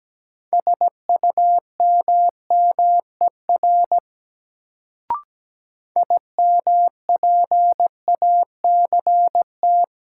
Morse code, heard as a keyed tone, about 17 wpm, 700 Hz. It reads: SUMMER IMPACT
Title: Sentences from Top 1000 Words - Review Part 6 - New Word Drill - 17wpm